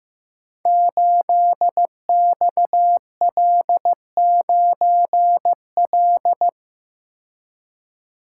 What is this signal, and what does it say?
Morse code, heard as a keyed tone, about 15 wpm, 700 Hz. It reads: 8XL9L
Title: Code Groups - 5 Letter:Numbers - 15wpm